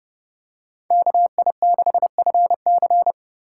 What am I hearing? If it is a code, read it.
Morse code, heard as a keyed tone, about 30 wpm, 700 Hz. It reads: KI6FC